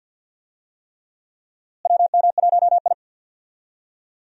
Morse code, heard as a keyed tone, about 49 wpm, 700 Hz. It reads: WM1I
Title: USA Call Signs - 4 Characters - Repeated 3x - 50wpm